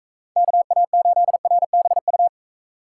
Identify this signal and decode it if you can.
Morse code, heard as a keyed tone, about 35 wpm, 700 Hz. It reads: KA8RBU